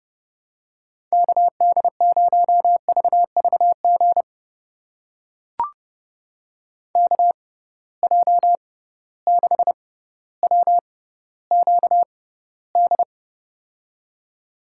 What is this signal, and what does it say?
Morse code, heard as a keyed tone, about 32 wpm, spaced as for 16 wpm, 700 Hz. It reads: KD0VVG KJ6WQD
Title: USA Call Signs - 6 Characters - 6x Character Spacing - 30wpm